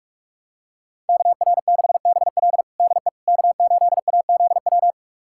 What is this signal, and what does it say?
Morse code, heard as a keyed tone, about 45 wpm, 700 Hz. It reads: KR6BL DE K8AZW
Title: USA Callsign DE Callsign - 5 Characters - 45wpm